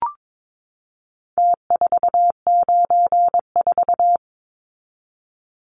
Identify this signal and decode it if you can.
Morse code, heard as a keyed tone, about 22 wpm, 700 Hz. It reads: T494
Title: Numbers - Contest Style - 4-Digit 22wpm